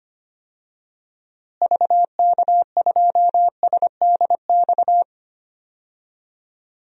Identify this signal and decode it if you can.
Morse code, heard as a keyed tone, about 25 wpm, 700 Hz. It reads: VK2SDX
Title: DXCC-100 Call Signs - 6 Characters - Repeated 3x - 25wpm